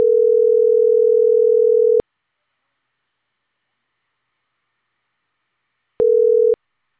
Ringback tone (North America), 440 + 480 Hz.